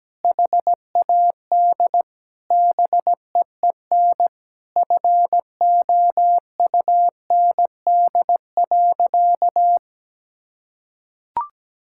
Morse code, heard as a keyed tone, about 17 wpm, 700 Hz. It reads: HAD BEEN FOUND.
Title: Sentences from Top 1000 Words - Review Part 5 - 17wpm